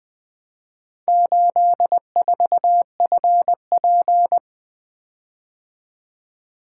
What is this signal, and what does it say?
Morse code, heard as a keyed tone, about 20 wpm, 700 Hz. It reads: 84FP
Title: Speed-Racing - Four Letter-Number Combinations - 20wpm